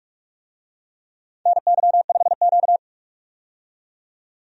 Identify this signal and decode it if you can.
Morse code, heard as a keyed tone, about 46 wpm, 700 Hz. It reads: NY5Q
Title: USA Call Signs - 4 Characters - 45wpm